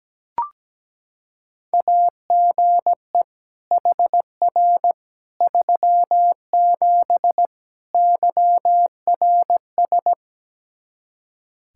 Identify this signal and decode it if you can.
Morse code, heard as a keyed tone, about 17 wpm, 700 Hz. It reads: AGE HR 37 YRS
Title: Practice QSO - Exchange 3 - More Info - 17wpm